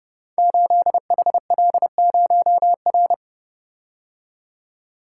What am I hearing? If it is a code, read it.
Morse code, heard as a keyed tone, about 29 wpm, 700 Hz. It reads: 8HL0R
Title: Code Groups - 5 Letter:Numbers - 30wpm